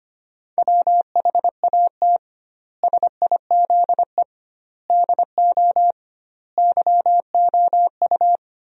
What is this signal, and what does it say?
Morse code, heard as a keyed tone, about 25 wpm, 700 Hz. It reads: WHAT SIZE DO YOU
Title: 2x Word-Spacing - Sentences from Top 400 Words - 25wpm